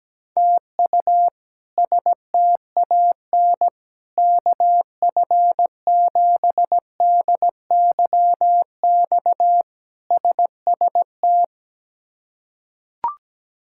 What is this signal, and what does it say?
Morse code, heard as a keyed tone, about 17 wpm, 700 Hz. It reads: TU STAN KF7DYX SST
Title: K1USN SST QSO Practice - 17wpm